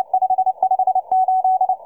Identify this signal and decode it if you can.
Morse code, heard as a keyed tone, about 30 wpm, 750 Hz. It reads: 558
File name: M12- -1700utc